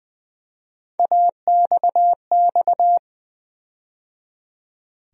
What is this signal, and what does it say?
Morse code, heard as a keyed tone, about 20 wpm, 700 Hz. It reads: AXX
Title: USA Call Sign Postfixes - 3 Letters 20wpm